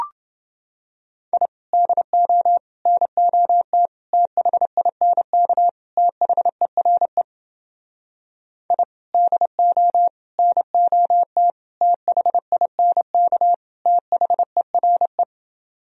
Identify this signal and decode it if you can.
Morse code, heard as a keyed tone, about 30 wpm, 700 Hz. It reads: I DO NOT THINK THERE I DO NOT THINK THERE